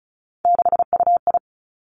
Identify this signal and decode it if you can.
Morse code, heard as a keyed tone, about 36 wpm, 700 Hz. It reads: 6UI
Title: Three Challenging Letter-Number Combinations 35wpm